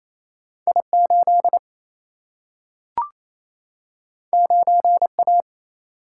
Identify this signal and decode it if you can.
Morse code, heard as a keyed tone, about 28 wpm, 700 Hz. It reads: I 8 9 A
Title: Single-Letter Single-Number 28wpm